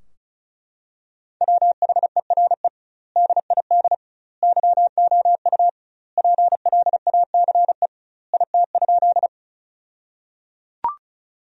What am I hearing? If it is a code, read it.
Morse code, heard as a keyed tone, about 36 wpm, 700 Hz. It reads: WHERE DID YOU PLACE IT?
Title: 2x Word-Spacing - Sentences from Top 200 Words - 35wpm